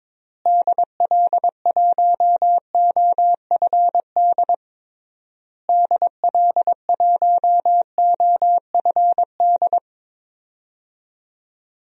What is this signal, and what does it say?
Morse code, heard as a keyed tone, about 22 wpm, 700 Hz. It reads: DL1OFD DL1OFD
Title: DXCC-100 Call Signs - 6 Characters - Repeated 3x - 22wpm